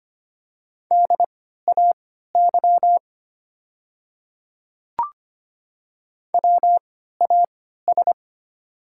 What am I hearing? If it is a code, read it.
Morse code, heard as a keyed tone, about 25 wpm, spaced as for 14 wpm, 700 Hz. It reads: DAY WAS
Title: 3x Character Spacing - Top 100 Words - 25wpm